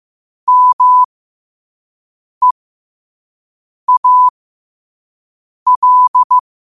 Morse code, heard as a keyed tone, about 15 wpm, spaced as for 5 wpm, 1000 Hz. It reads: MEAL
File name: qotdfiles / 5wpm